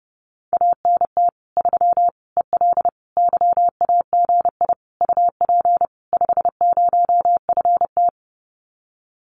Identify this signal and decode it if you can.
Morse code, heard as a keyed tone, about 30 wpm, 700 Hz. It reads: ANT 3 EL YAGI UP 50FT
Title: Common QSO Phrases Encore 30wpm